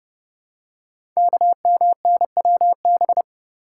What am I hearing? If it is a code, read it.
Morse code, heard as a keyed tone, about 30 wpm, 700 Hz. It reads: KMNWB
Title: Code Groups - 5 Letters - 30wpm